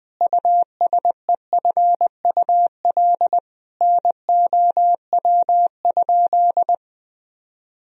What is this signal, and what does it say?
Morse code, heard as a keyed tone, about 21 wpm, 700 Hz. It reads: USEFUL NOW?
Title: Sentences from Top 1000 Words - Review Part 4 - 20wpm